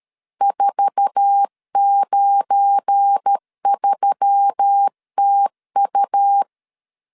Morse code, heard as a keyed tone, about 13 wpm, 800 Hz. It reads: 493TU
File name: LEZIONI / 8 YJ45 / 8 YJ4 12wpm 2151